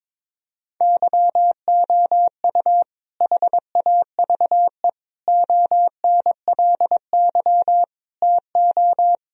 Morse code, heard as a keyed tone, about 22 wpm, 700 Hz. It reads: YOU HAVE ONLY TO
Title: Sentences from Top 1000 Words - Review Part 1 - 22wpm